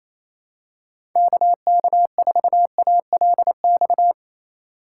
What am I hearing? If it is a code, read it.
Morse code, heard as a keyed tone, about 28 wpm, 700 Hz. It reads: KK4ALX
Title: USA Call Signs - 6 Characters - 6x Character Spacing - 28wpm